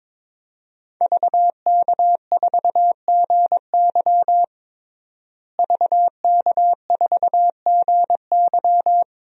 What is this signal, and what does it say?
Morse code, heard as a keyed tone, about 22 wpm, 700 Hz. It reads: VK4GY VK4GY
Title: DXCC-100 Call Signs - 5 Characters - Repeated 2x - 22wpm